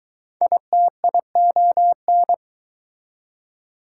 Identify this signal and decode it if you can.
Morse code, heard as a keyed tone, about 21 wpm, 700 Hz. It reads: ITION